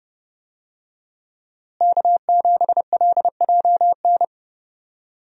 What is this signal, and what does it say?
Morse code, heard as a keyed tone, about 30 wpm, 700 Hz. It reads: K7LJN
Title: USA Call Signs - 5 Characters - Repeated 2x - 30wpm